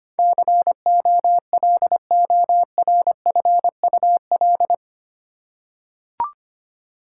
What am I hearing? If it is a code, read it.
Morse code, heard as a keyed tone, about 25 wpm, 700 Hz. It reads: COLORFUL
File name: Eight Letter Words - 25wpm